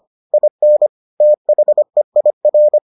Morse code, heard as a keyed tone, about 25 wpm, 600 Hz. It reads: IN THEIR